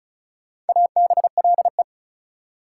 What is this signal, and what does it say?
Morse code, heard as a keyed tone, about 36 wpm, 700 Hz. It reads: ABLE
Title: Prefixes and Suffixes 35wpm